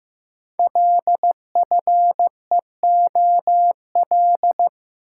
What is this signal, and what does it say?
Morse code, heard as a keyed tone, about 15 wpm, 700 Hz. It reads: LFEOL
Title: Code Groups - 5 Letters - 15wpm